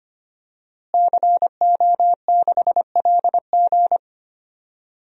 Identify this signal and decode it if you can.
Morse code, heard as a keyed tone, about 25 wpm, 700 Hz. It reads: CO6LG
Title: DXCC-100 Call Signs - 5 Characters - Repeated 3x - 25wpm